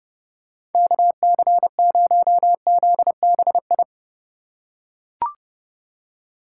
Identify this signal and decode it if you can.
Morse code, heard as a keyed tone, about 28 wpm, 700 Hz. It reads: KC0ZBI